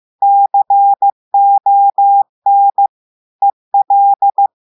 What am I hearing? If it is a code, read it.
Morse code, heard as a keyed tone, about 15 wpm, 800 Hz. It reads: CON EL
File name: eh1ard-amateur-radio-day-2016 mf feed 1